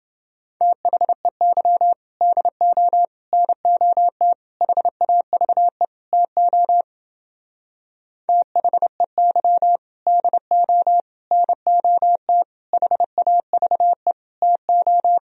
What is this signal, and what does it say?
Morse code, heard as a keyed tone, about 31 wpm, 700 Hz. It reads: THEY DO NOT HAVE TO THEY DO NOT HAVE TO